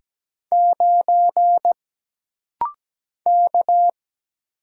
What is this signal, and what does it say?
Morse code, heard as a keyed tone, about 17 wpm, 700 Hz. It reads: 9 K
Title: Single Letter-Number - Rapid-Fire 17wpm